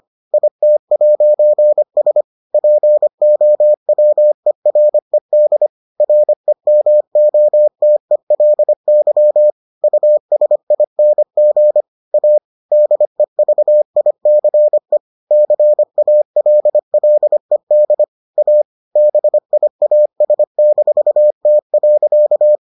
Morse code, heard as a keyed tone, about 25 wpm, 600 Hz. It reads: IT'S POWERED REMOTELY USING A DEVICE CALLED A BIAS-T.